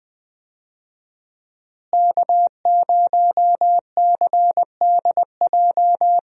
Morse code, heard as a keyed tone, about 20 wpm, 700 Hz. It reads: K0CDJ
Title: USA Call Signs - 5 Characters - Repeated 2x - 20wpm